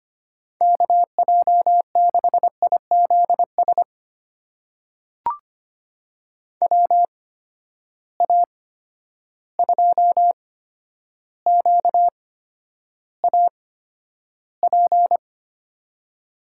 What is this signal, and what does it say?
Morse code, heard as a keyed tone, about 25 wpm, spaced as for 11 wpm, 700 Hz. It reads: KJ6IZS WA2QAP
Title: USA Call Signs - 6 Characters - 8x Character Spacing - 25wpm